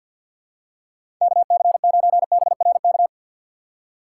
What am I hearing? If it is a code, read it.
Morse code, heard as a keyed tone, about 47 wpm, 700 Hz. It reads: KC9BRK
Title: USA Call Signs - 4x Character Spacing - 50wpm